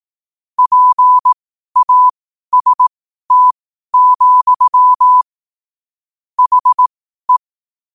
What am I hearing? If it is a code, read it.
Morse code, heard as a keyed tone, about 18 wpm, spaced as for 12 wpm, 1000 Hz. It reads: PAST, HE
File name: qotdfiles / 12wpm